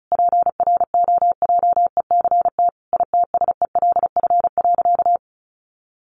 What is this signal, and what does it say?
Morse code, heard as a keyed tone, about 34 wpm, 700 Hz. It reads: PROJECT ITSELF.
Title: Sentences from Top 1400 Words 35wpm